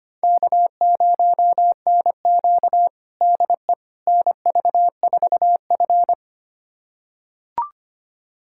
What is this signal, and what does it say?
Morse code, heard as a keyed tone, about 25 wpm, 700 Hz. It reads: K0NQ DE NV4F
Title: USA Callsign DE Callsign - 4 Characters - 25wpm